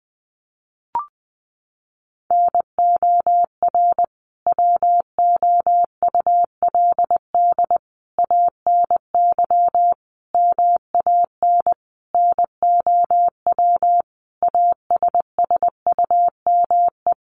Morse code, heard as a keyed tone, about 19 wpm, 700 Hz. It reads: NOR WOULD ANY MAN NOW ASSUME
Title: Sentences from Top 1500 Words 20wpm